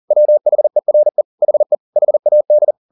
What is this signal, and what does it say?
Morse code, heard as a keyed tone, about 40 wpm, 600 Hz. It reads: WHERE HE HAD
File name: the old man and the sea 2 40wpm